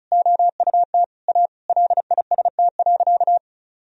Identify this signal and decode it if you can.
Morse code, heard as a keyed tone, about 36 wpm, 700 Hz. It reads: OUT A LIST.
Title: Sentences from Top 1000 Words - Review Part 2 - 35wpm